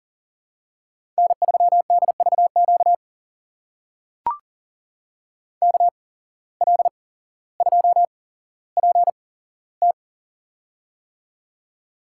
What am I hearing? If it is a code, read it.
Morse code, heard as a keyed tone, about 39 wpm, spaced as for 17 wpm, 700 Hz. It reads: N3DVQ KL2PT
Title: USA Call Signs - 5 Characters - 8x Character Spacing - 40wpm